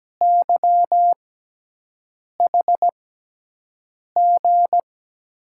Morse code, heard as a keyed tone, about 17 wpm, spaced as for 6 wpm, 700 Hz. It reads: YHG
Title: USA Call Signs - 5 Characters - 6x Character Spacing - 17wpm